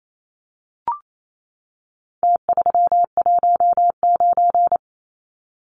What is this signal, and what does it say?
Morse code, heard as a keyed tone, about 28 wpm, 700 Hz. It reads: T319
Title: Numbers - Contest Style - 4-Digit 28wpm